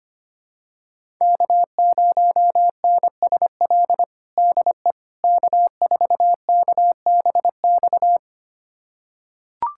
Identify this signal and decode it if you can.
Morse code, heard as a keyed tone, about 25 wpm, 700 Hz. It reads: K0NSL DE K4KBX
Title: USA Callsign DE Callsign - 5 Characters - 25wpm